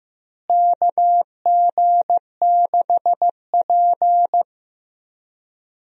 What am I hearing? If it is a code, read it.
Morse code, heard as a keyed tone, about 15 wpm, 700 Hz. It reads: KG6P
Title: USA Call Signs - 4 Characters - Repeated 2x - 15wpm